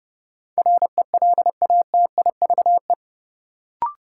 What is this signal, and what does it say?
Morse code, heard as a keyed tone, about 31 wpm, 700 Hz. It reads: RELATIVE
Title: Sentences from Top 1400 Words - New Word Drill Alt - 30wpm